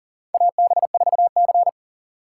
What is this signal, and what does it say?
Morse code, heard as a keyed tone, about 40 wpm, 700 Hz. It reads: AB4C
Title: USA Call Signs - 4 Characters - 4x Character Spacing - 40wpm